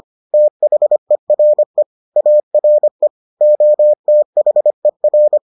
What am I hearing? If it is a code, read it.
Morse code, heard as a keyed tone, about 25 wpm, 600 Hz. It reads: THERE ARE OTHER